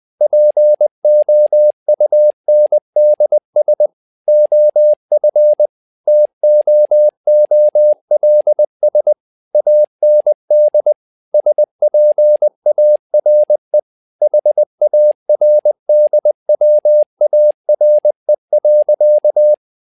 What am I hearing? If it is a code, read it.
Morse code, heard as a keyed tone, about 20 wpm, 600 Hz. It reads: POUNDS OF TOOLS AND SPARE HARDWARE.
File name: tower trouble 20wpm